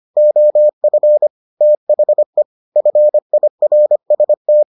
Morse code, heard as a keyed tone, about 25 wpm, 600 Hz. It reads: OF THE FIRST